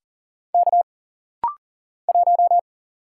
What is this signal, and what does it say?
Morse code, heard as a keyed tone, about 40 wpm, 700 Hz. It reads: K 1
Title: Single Letter-Number - Rapid-Fire 40wpm